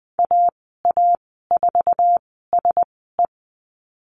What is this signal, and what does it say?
Morse code, heard as a keyed tone, about 20 wpm, spaced as for 15 wpm, 700 Hz. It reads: AA4SE
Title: 147 Lesson 24g - Letter y - Call Signs 20wpm